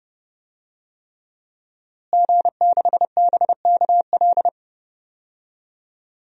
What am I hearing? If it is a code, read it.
Morse code, heard as a keyed tone, about 32 wpm, 700 Hz. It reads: G6BKL
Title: DXCC-100 Call Signs - 5 Characters - Repeated 3x - 30wpm